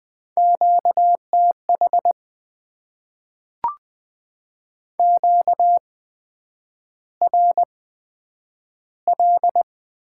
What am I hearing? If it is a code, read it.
Morse code, heard as a keyed tone, about 20 wpm, spaced as for 9 wpm, 700 Hz. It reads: QTH QRL
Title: Top 20 QSO Elements - 8x Character Spacing - 20wpm